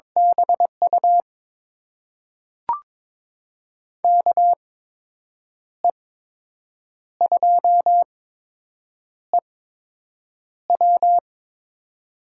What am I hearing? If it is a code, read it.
Morse code, heard as a keyed tone, about 21 wpm, spaced as for 7 wpm, 700 Hz. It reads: BU KE2EW
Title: USA Call Signs - 8x Character Spacing - 22wpm